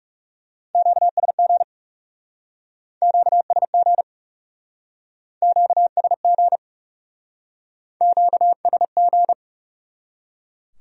Morse code, heard as a keyed tone, about 38 wpm, 700 Hz. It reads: QSG QSG QSG QSG